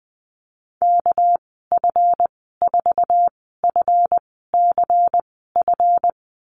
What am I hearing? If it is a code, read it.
Morse code, heard as a keyed tone, about 20 wpm, spaced as for 14 wpm, 700 Hz. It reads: KF4FCF
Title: 238 Lesson 38g - Letter q - Call Signs 20wpm